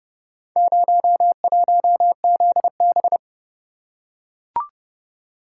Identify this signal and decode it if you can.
Morse code, heard as a keyed tone, about 29 wpm, 700 Hz. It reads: 01ZB